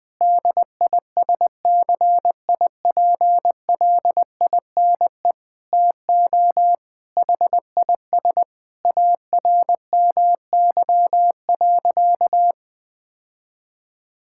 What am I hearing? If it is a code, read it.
Morse code, heard as a keyed tone, about 20 wpm, 700 Hz. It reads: DISCIPLINE TO HIS ARMY.
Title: Sentences from Top 1000 Words - Review Part 6 - 20wpm